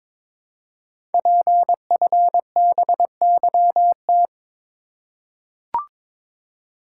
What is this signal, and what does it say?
Morse code, heard as a keyed tone, about 22 wpm, 700 Hz. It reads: PFBYT
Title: Code Groups - 5 Letters - 22wpm